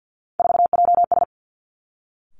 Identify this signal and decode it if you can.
Morse code, heard as a keyed tone, about 50 wpm, 700 Hz. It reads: 4JS
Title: Three Challenging Letter-Number Combinations 50wpm